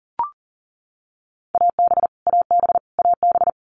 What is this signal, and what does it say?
Morse code, heard as a keyed tone, about 40 wpm, 700 Hz. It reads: AB AB AB
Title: Canadian Provinces and Territories - Repeated 3x - 40wpm